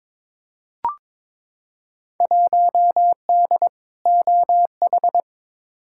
Morse code, heard as a keyed tone, about 22 wpm, 700 Hz. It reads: 1D OH
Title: Field Day Exchanges - 22wpm